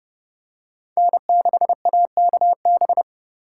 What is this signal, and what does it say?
Morse code, heard as a keyed tone, about 29 wpm, 700 Hz. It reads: N6AKB